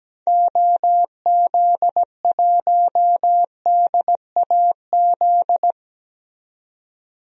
Morse code, heard as a keyed tone, about 17 wpm, 700 Hz. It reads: OZ1DAZ
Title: DXCC-100 Call Signs - 6 Characters - 17wpm